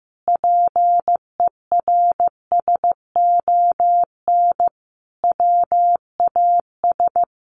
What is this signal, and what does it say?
Morse code, heard as a keyed tone, about 15 wpm, 700 Hz. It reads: PERSON WAS
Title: Sentences from Top 1400 Words 15wpm